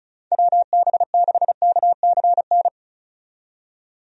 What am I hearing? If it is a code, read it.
Morse code, heard as a keyed tone, about 36 wpm, 700 Hz. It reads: WB6KCN
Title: USA Call Signs - 6 Characters - 35wpm